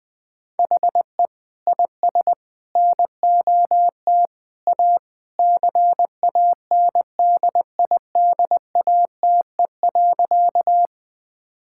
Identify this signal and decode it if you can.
Morse code, heard as a keyed tone, about 20 wpm, 700 Hz. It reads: HE IS NOT A CANDIDATE.
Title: Sentences from Top 1600 Words 20wpm